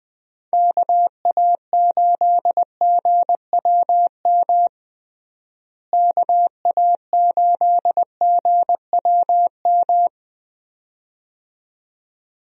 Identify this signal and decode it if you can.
Morse code, heard as a keyed tone, about 19 wpm, 700 Hz. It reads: KA8GWM KA8GWM
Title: USA Call Signs - 6 Characters - Repeated 2x - 20wpm